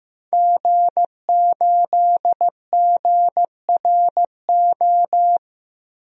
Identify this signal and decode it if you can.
Morse code, heard as a keyed tone, about 15 wpm, 700 Hz. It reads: G8GRO
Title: DXCC-100 Call Signs - 5 Characters - 15wpm